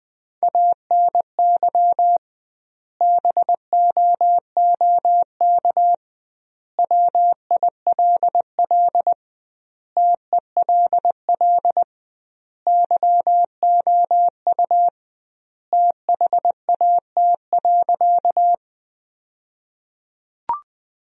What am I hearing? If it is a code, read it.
Morse code, heard as a keyed tone, about 19 wpm, 700 Hz. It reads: ANY BOOK WILL TELL YOU THAT.
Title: 2x Word-Spacing - Sentences from Top 300 Words - 20wpm